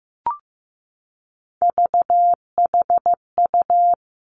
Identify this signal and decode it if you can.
Morse code, heard as a keyed tone, about 15 wpm, 700 Hz. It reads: VHU
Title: Three Challenging Letter-Number Combinations 15wpm